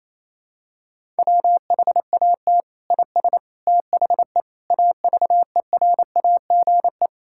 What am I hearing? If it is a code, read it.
Morse code, heard as a keyed tone, about 28 wpm, 700 Hz. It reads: WHAT IS THE AVERAGE
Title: Sentences from Top 1000 Words - Review Part 6 - 28wpm